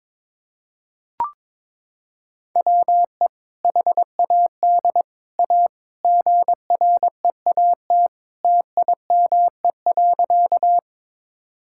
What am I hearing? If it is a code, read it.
Morse code, heard as a keyed tone, about 22 wpm, 700 Hz. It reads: WE HAD A GREAT TIME.
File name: practice / 22wpm / Sentences with 2-Word Phrases - 22wpm